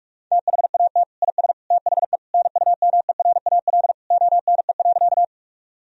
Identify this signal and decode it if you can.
Morse code, heard as a keyed tone, about 46 wpm, 700 Hz. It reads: THAT IS THE NUMERAL ONE.
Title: Sentences from Top 1000 Words - Review Part 2 - 45wpm